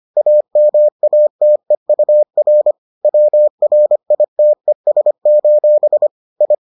Morse code, heard as a keyed tone, about 25 wpm, 600 Hz. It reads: AMATEUR WRITES: I